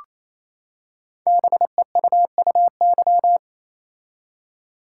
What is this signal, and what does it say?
Morse code, heard as a keyed tone, about 29 wpm, 700 Hz. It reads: BEUUY
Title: Code Groups - 5 Letters - 28wpm